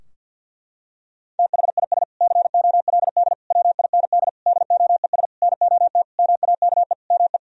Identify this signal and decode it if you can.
Morse code, heard as a keyed tone, about 50 wpm, 700 Hz. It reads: THIS COLD WIND DOES NOT MAKE ME
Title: Sentences from Top 1000 Words - Review Part 2 - 50wpm